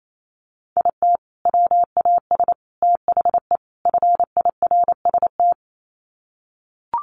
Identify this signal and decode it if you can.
Morse code, heard as a keyed tone, about 28 wpm, 700 Hz. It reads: IT WAS THE FIRST
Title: Sets of 4 Words 28wpm